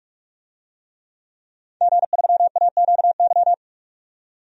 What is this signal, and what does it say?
Morse code, heard as a keyed tone, about 45 wpm, 700 Hz. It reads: G3AQY
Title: DXCC-100 Call Signs - 5 Characters - 45wpm